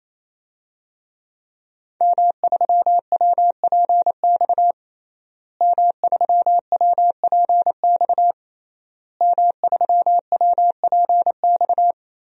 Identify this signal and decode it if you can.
Morse code, heard as a keyed tone, about 28 wpm, 690 Hz. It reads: M3WPX M3WPX M3WPX
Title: DXCC-100 Call Signs - 5 Characters - Repeated 3x - 28wpm